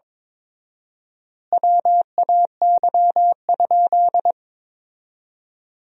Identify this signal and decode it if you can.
Morse code, heard as a keyed tone, about 22 wpm, 700 Hz. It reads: WAY?
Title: 4x Word-Spacing - Sentences from Top 200 Words - 22wpm